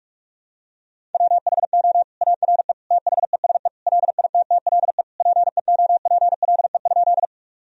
Morse code, heard as a keyed tone, about 46 wpm, 700 Hz. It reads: WHO ARE THESE LITTLE PEOPLE?
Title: Sentences from Top 1000 Words - Review Part 1 - 45wpm